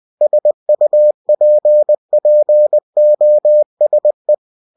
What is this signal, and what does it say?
Morse code, heard as a keyed tone, about 20 wpm, 600 Hz. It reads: SUPPOSE